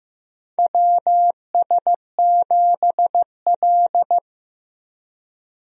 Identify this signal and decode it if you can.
Morse code, heard as a keyed tone, about 16 wpm, 700 Hz. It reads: WS7L
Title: USA Call Signs - 4 Characters - Repeated 2x - 15wpm